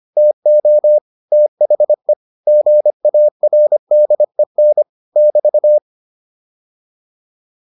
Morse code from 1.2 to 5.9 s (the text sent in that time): THE GARDEN <BT>